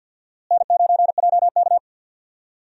Morse code, heard as a keyed tone, about 51 wpm, 700 Hz. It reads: N9JK
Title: USA Call Signs - 4 Characters - Repeated 2x - 50wpm